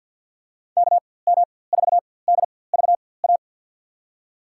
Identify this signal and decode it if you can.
Morse code, heard as a keyed tone, about 53 wpm, spaced as for 24 wpm, 700 Hz. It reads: KM4DVA
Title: USA Call Signs - 4x Character Spacing - 50wpm